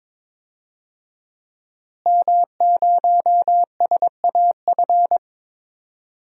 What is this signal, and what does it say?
Morse code, heard as a keyed tone, about 21 wpm, 700 Hz. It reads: M0SAF
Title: DXCC-100 Call Signs - 5 Characters - Repeated 2x - 22wpm